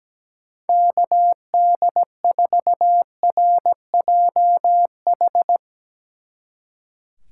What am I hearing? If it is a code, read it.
Morse code, heard as a keyed tone, about 18 wpm, 700 Hz. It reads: KD4RJH
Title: Speed Racing - USA 2x3 Call Signs 17wpm